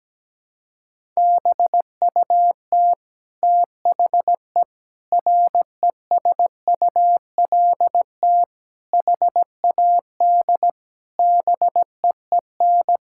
Morse code, heard as a keyed tone, about 17 wpm, 700 Hz. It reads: BUT THE RESULT HAD BEEN
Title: Sentences from Top 1000 Words - Review Part 5 - 17wpm